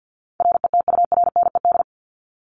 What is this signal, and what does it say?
Morse code, heard as a keyed tone, about 50 wpm, 700 Hz. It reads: RETURNED
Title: ED Suffix - Top 100 Words with Base Words 50wpm